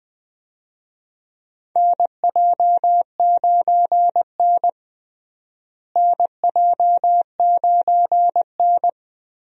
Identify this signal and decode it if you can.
Morse code, heard as a keyed tone, about 20 wpm, 700 Hz. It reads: NJ9N NJ9N
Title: USA Call Signs - 4 Characters - Repeated 2x - 20wpm